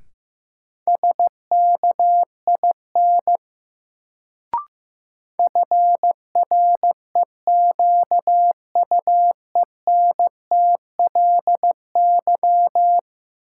Morse code, heard as a keyed tone, about 15 wpm, 700 Hz. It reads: SKIN FREQUENTLY
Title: Sentences from Top 1300 Words - New Word Drill - 15wpm